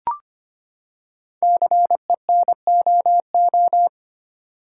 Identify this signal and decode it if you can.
Morse code, heard as a keyed tone, about 25 wpm, 700 Hz. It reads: CENOO
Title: Code Groups - 5 Letters - 25wpm